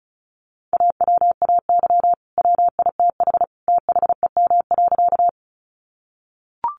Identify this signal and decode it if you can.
Morse code, heard as a keyed tone, about 34 wpm, 700 Hz. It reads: AWAY WITH THEM.
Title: Sentences from Top 600 Words 35wpm